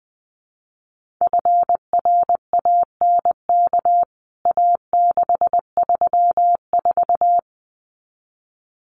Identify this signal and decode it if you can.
Morse code, heard as a keyed tone, about 20 wpm, 700 Hz. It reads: FRANK A634
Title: CWT Exchange 20wpm